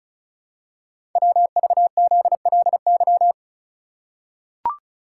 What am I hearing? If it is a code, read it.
Morse code, heard as a keyed tone, about 36 wpm, 700 Hz. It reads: WVZLY
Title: Code Groups - 5 Letters - 35wpm